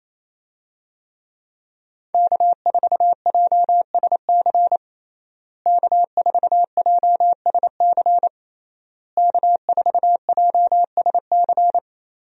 Morse code, heard as a keyed tone, about 28 wpm, 700 Hz. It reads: K4JSC K4JSC K4JSC
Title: USA Call Signs - 5 Characters - Repeated 3x - 28wpm